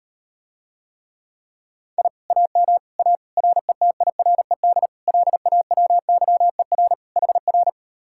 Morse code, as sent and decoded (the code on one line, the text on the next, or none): .. / .- -- / .- / .-. - .. .-. -.. / .-.. .- .-- -.-- . .-. / .... .-.
I AM A RETIRED LAWYER HR